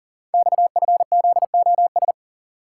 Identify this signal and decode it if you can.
Morse code, heard as a keyed tone, about 40 wpm, 700 Hz. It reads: XFZOS
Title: Code Groups - 5 Letters - 40wpm